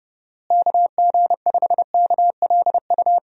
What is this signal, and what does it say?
Morse code, heard as a keyed tone, about 30 wpm, 700 Hz. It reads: KG5KLU
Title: USA Call Signs - 6 Characters - Repeated 2x - 30wpm